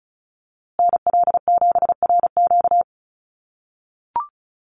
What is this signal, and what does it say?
Morse code, heard as a keyed tone, about 34 wpm, 700 Hz. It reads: NL7RQ
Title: USA Call Signs Repeated 3x 35wpm